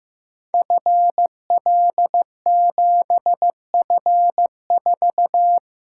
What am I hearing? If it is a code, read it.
Morse code, heard as a keyed tone, about 15 wpm, 700 Hz. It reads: FL7F4
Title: Code Groups - 5 Letter:Numbers - 15wpm